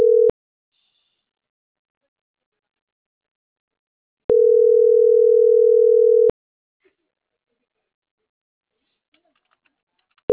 Ringback tone (North America), 440 + 480 Hz.